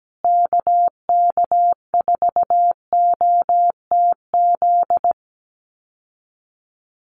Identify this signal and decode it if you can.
Morse code, heard as a keyed tone, about 17 wpm, 700 Hz. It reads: KK4OTZ
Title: USA Call Signs Repeated 3x 17wpm